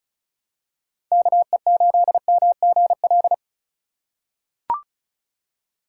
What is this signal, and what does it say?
Morse code, heard as a keyed tone, about 36 wpm, 700 Hz. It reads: KE8MGL
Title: USA Call Signs - 8x Character Spacing - 35wpm